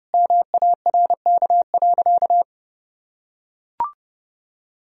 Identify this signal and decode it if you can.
Morse code, heard as a keyed tone, about 30 wpm, 700 Hz. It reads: MARK.
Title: Sentences from Top 1000 Words - Review Part 1 - 30wpm